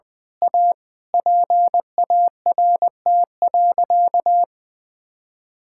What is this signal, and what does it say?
Morse code, heard as a keyed tone, about 21 wpm, 700 Hz. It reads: A PART.
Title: Sentences from Top 1000 Words - Review Part 5 - 20wpm